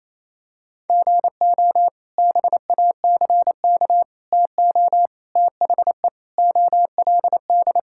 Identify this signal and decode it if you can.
Morse code, heard as a keyed tone, about 28 wpm, 700 Hz. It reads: GO BACK TO THE OLD
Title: Sentences from Top 1000 Words - Review Part 5 - 28wpm